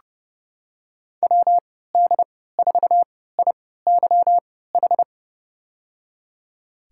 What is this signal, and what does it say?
Morse code, heard as a keyed tone, about 30 wpm, spaced as for 17 wpm, 700 Hz. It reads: WD4IYH
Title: USA Call Signs - 6 Characters - 3x Character Spacing - 30wpm